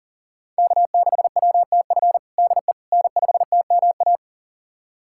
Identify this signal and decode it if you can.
Morse code, heard as a keyed tone, about 37 wpm, 700 Hz. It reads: K6WTF DE N5TMA
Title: USA Callsign DE Callsign - 5 Characters - 40wpm